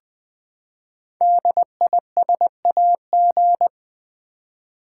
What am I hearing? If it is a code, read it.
Morse code, heard as a keyed tone, about 20 wpm, 700 Hz. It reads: DISAG
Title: Build A Word - 5 Letters At A Time - 12-Letter Words - 20wpm